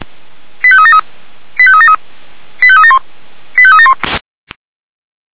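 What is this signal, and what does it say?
Five-tone selective call (ZVEI), heard as quick runs of five tones, tones of about 70 ms. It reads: 85373 85373 85371 85371
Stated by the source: ZVEI five-tone call 85381 received by SDR, channel 464.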